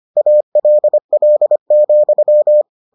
Morse code, heard as a keyed tone, about 25 wpm, 600 Hz. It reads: ALL,